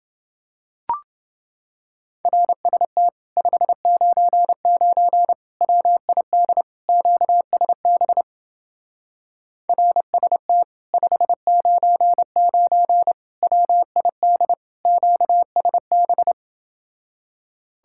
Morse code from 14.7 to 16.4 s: QSB

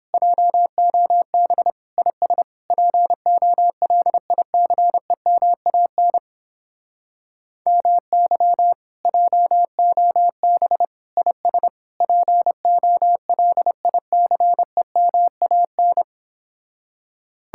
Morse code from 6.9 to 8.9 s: MY

